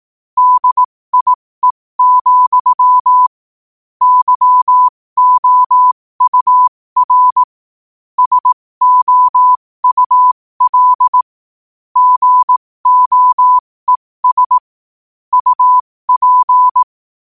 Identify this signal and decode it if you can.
Morse code, heard as a keyed tone, about 18 wpm, spaced as for 15 wpm, 1000 Hz. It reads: DIE, YOUR SOUL GOES UP
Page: qotd-at-15-wpm